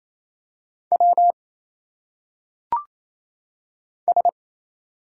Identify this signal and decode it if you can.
Morse code, heard as a keyed tone, about 27 wpm, 700 Hz. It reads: W S
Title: Single Letters - Phonetic Pronunciation 28wpm